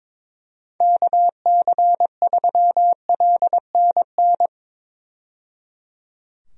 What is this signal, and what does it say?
Morse code, heard as a keyed tone, about 22 wpm, 700 Hz. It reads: KC3LNN
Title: USA Call Signs - 6 Characters - 22wpm